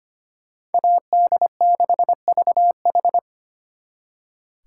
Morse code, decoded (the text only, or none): AD6VH